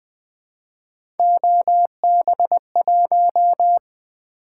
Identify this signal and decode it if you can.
Morse code, heard as a keyed tone, about 20 wpm, 700 Hz. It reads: OB1
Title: Three Letter-Number Combinations - 20wpm